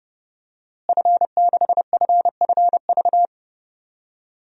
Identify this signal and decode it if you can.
Morse code, heard as a keyed tone, about 30 wpm, 700 Hz. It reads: F6FFV
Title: DXCC-100 Call Signs - 5 Characters - Repeated 2x - 30wpm